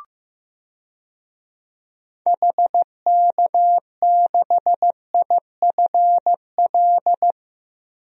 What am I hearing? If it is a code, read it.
Morse code, heard as a keyed tone, about 15 wpm, 700 Hz. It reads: HK6IFL
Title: DXCC-100 Call Signs - 6 Characters - Repeated 2x - 15wpm